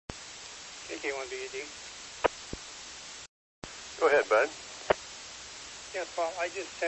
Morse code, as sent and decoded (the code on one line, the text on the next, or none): none